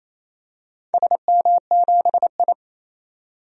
Morse code, heard as a keyed tone, about 27 wpm, 700 Hz. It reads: SM7I
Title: DXCC-100 Call Signs - 4 Characters - 28wpm